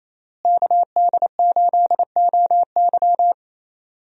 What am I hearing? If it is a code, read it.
Morse code, heard as a keyed tone, about 28 wpm, 700 Hz. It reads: KD8OY
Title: USA Call Signs - 5 Characters - Repeated 3x - 28wpm